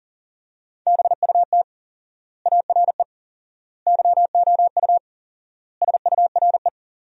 Morse code, heard as a keyed tone, about 40 wpm, 700 Hz. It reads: BUT ARE YOU SURE
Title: 4x Word-Spacing - Sentences from Top 300 Words - 40wpm